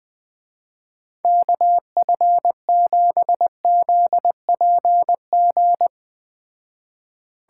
Morse code, heard as a keyed tone, about 21 wpm, 700 Hz. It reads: KF7ZPG
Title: Speed Racing - USA 2x3 Call Signs 17wpm